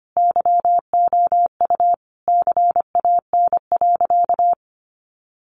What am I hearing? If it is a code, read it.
Morse code, heard as a keyed tone, about 25 wpm, 700 Hz. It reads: YOU CAN.
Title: Sentences from Top 1400 Words 25wpm